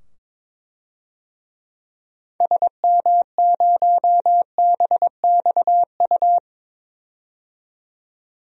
Morse code, heard as a keyed tone, about 22 wpm, 700 Hz. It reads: SM0BXU
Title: DXCC-100 Call Signs - 6 Characters - 22wpm